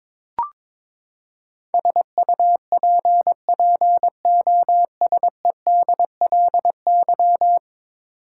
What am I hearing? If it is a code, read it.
Morse code, heard as a keyed tone, about 22 wpm, 700 Hz. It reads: SUPPOSEDLY
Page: morse-code-ninja-practice-22wpm